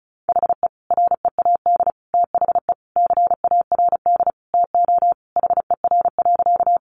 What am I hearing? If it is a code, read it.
Morse code, heard as a keyed tone, about 34 wpm, 700 Hz. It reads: HE READ THE CARD TO HER.
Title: Sentences from Top 600 Words 35wpm